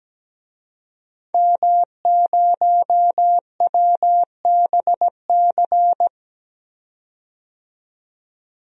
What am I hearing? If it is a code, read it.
Morse code, heard as a keyed tone, about 17 wpm, 700 Hz. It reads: M0WBC
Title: DXCC-100 Call Signs - 5 Characters - 17wpm